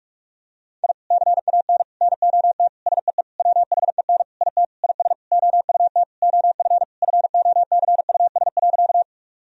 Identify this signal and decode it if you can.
Morse code, heard as a keyed tone, about 41 wpm, 700 Hz. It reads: I CAN NOT SEE WHEN IT IS OUT OF FOCUS.
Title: Sentences from Top 1000 Words - Review Part 4 - 45wpm